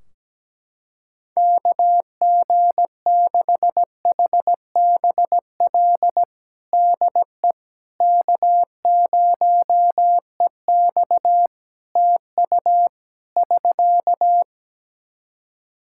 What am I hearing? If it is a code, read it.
Morse code, heard as a keyed tone, about 17 wpm, 700 Hz. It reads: KG6HBL DE K0EX TU <SK>
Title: K1USN SST QSO Practice 17wpm